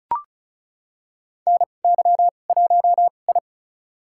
Morse code, heard as a keyed tone, about 34 wpm, spaced as for 25 wpm, 700 Hz. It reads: NY1I
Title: USA Call Signs - 2x Character Spacing - 35wpm